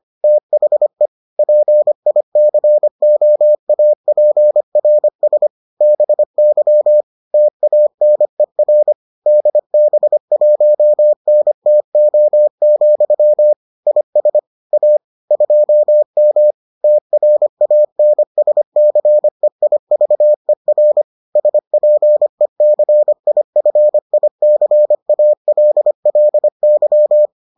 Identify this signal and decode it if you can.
Morse code, heard as a keyed tone, about 25 wpm, 600 Hz. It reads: THE PICOAPRS BY TANER DB1NTO, IS A 2M TRANSCEIVER SPECIFICALLY